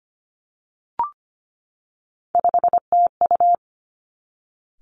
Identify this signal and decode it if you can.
Morse code, heard as a keyed tone, about 25 wpm, 700 Hz. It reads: 5TU
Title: Three Challenging Letter-Number Combinations 25wpm